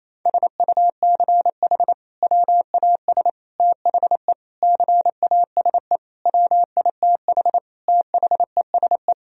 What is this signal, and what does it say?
Morse code, heard as a keyed tone, about 28 wpm, 700 Hz. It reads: SUCH WAS THE CASE WITH THESE
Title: Sentences from Top 1000 Words - Review Part 1 - 28wpm